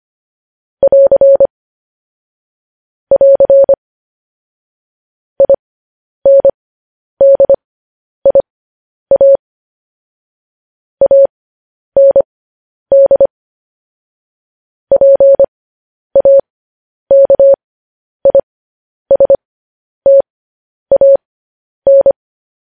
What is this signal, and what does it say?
Morse code, heard as a keyed tone, about 25 wpm, spaced as for 10 wpm, 550 Hz. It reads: <AR> <AR> INDIA AND PAKISTAN
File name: podcast / News headlines-1x-25x10